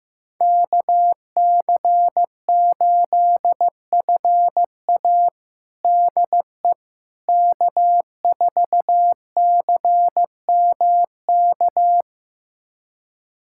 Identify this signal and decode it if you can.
Morse code, heard as a keyed tone, about 15 wpm, 700 Hz. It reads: KC8FA DE K4CMK
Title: USA Callsign DE Callsign - 5 Characters - 15wpm